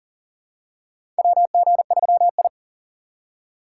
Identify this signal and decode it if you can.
Morse code, heard as a keyed tone, about 40 wpm, 700 Hz. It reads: WG3I
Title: USA Call Signs - 4x Character Spacing - 40wpm